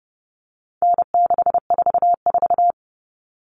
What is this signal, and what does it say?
Morse code, heard as a keyed tone, about 30 wpm, 700 Hz. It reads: N644
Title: Common Cut Numbers - 4-Digit 30wpm